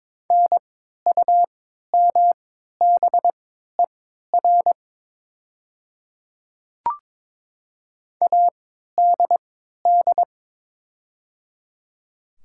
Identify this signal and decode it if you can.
Morse code, heard as a keyed tone, about 22 wpm, spaced as for 12 wpm, 700 Hz. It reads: NUMBER ADD
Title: 3x Character Spacing - Top 200 Words - 22wpm